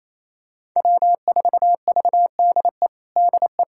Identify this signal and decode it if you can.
Morse code, heard as a keyed tone, about 28 wpm, 700 Hz. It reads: W4VDE DE
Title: USA Callsign DE Callsign - 5 Characters - 28wpm